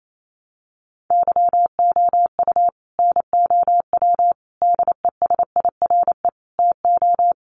Morse code, heard as a keyed tone, about 28 wpm, 700 Hz. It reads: YOU NOW DESIRE TO
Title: Sentences from Top 1100 Words 28wpm